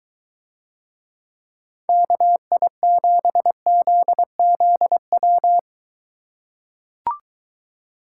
Morse code, heard as a keyed tone, about 22 wpm, 700 Hz. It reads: KI7ZZW